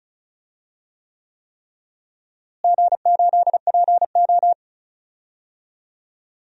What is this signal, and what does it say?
Morse code, heard as a keyed tone, about 36 wpm, 700 Hz. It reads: G8PO
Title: DXCC-100 Call Signs - 4 Characters - Repeated 3x - 35wpm